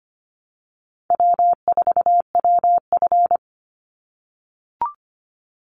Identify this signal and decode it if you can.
Morse code, heard as a keyed tone, about 25 wpm, 700 Hz. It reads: W4WF
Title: Call Signs - 25wpm